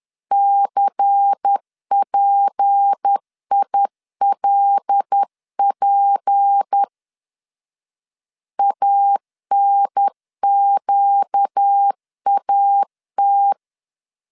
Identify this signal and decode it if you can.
Morse code, heard as a keyed tone, about 10 wpm, 800 Hz. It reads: CPILP ANQAT
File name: LEZIONI / 6 PKX0 / 6 P 10wpm 2432